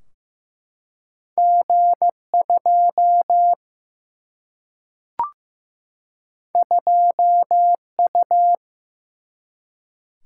Morse code, heard as a keyed tone, about 15 wpm, 700 Hz. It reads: G 2 2 U
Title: Single-Letter Single-Number 15wpm